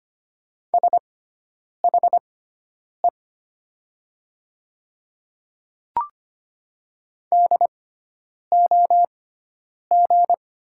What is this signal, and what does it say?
Morse code, heard as a keyed tone, about 25 wpm, spaced as for 9 wpm, 700 Hz. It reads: SHE DOG
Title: 6x Character Spacing - Top 500 Words - 25wpm